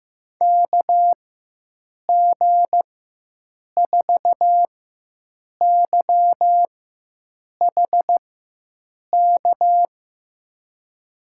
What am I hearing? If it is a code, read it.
Morse code, heard as a keyed tone, about 15 wpm, spaced as for 7 wpm, 700 Hz. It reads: KG4YHK
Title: USA Call Signs - 6 Characters - 4x Character Spacing - 15wpm